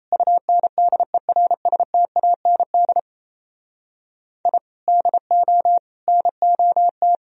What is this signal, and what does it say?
Morse code, heard as a keyed tone, about 35 wpm, 700 Hz. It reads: UNDERSTAND I DO NOT